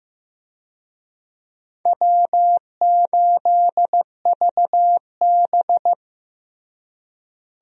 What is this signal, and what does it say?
Morse code, heard as a keyed tone, about 15 wpm, 700 Hz. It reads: W8VB
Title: USA Call Signs - 4 Characters - Repeated 3x - 15wpm